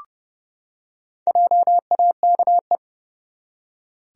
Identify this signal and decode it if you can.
Morse code, heard as a keyed tone, about 29 wpm, 700 Hz. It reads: JAKE
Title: USA Common Names - 30wpm